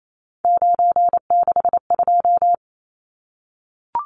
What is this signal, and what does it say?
Morse code, heard as a keyed tone, about 28 wpm, 700 Hz. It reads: 962
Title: Numbers - 3-Digit 28wpm